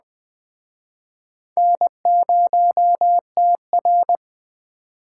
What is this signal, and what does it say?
Morse code, heard as a keyed tone, about 21 wpm, 700 Hz. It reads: N0TR